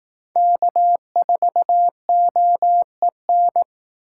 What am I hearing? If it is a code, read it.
Morse code, heard as a keyed tone, about 18 wpm, 700 Hz. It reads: K4OEN
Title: Speed Racing - USA 1x3 Call Signs 15wpm